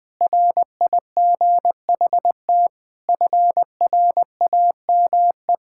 Morse code, heard as a keyed tone, about 21 wpm, 700 Hz. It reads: RIGHT FRAME
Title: Sentences from Top 1000 Words - Review Part 6 - 20wpm